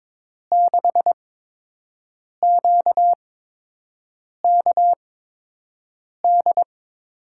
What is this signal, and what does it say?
Morse code, heard as a keyed tone, about 22 wpm, spaced as for 6 wpm, 700 Hz. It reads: 6QKD
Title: USA Call Signs - 6 Characters - 8x Character Spacing - 22wpm